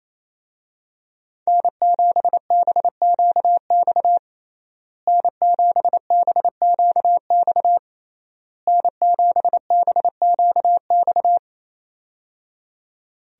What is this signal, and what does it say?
Morse code, heard as a keyed tone, about 28 wpm, 700 Hz. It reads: N7BQX N7BQX N7BQX
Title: USA Call Signs - 5 Characters - Repeated 3x - 28wpm